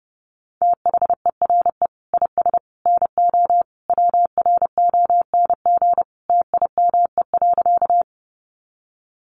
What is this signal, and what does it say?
Morse code, heard as a keyed tone, about 28 wpm, 700 Hz. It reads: THERE IS NO WRONG TIME.
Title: Sentences from Top 900 Words 30wpm